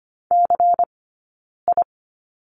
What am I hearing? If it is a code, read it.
Morse code, heard as a keyed tone, about 25 wpm, 700 Hz. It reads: C I
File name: course / 25:10wpm / 092 Lesson 16d - Letter c - Two Characters 25wpm